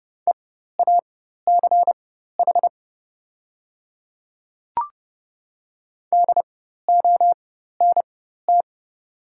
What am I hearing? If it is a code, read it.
Morse code, heard as a keyed tone, about 30 wpm, spaced as for 14 wpm, 700 Hz. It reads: EACH DONT
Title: 4x Character Spacing - Top 500 Words - 30wpm